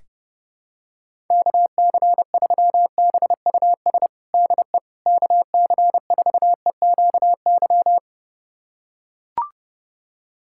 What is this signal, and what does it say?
Morse code, heard as a keyed tone, about 32 wpm, 700 Hz. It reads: KC3BUS DE KC4EQY
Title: USA Callsign DE Callsign 30wpm